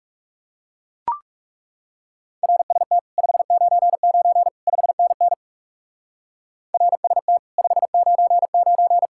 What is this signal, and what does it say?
Morse code, heard as a keyed tone, about 36 wpm, 700 Hz. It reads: RST 599 5NN RST 599
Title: Speed-Racing Common QSO Phrases - 30wpm